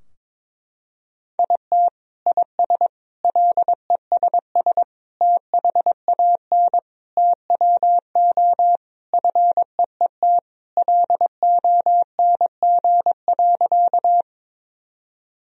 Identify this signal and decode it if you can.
Morse code, heard as a keyed tone, about 22 wpm, 700 Hz. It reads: IT IS LESS THAN TWO FEET LONG.
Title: Sentences from Top 1000 Words - Review Part 2 - 22wpm